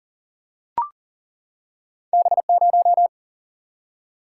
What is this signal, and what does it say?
Morse code, heard as a keyed tone, about 40 wpm, 700 Hz. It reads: B0
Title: Single-Letter Single-Number 40wpm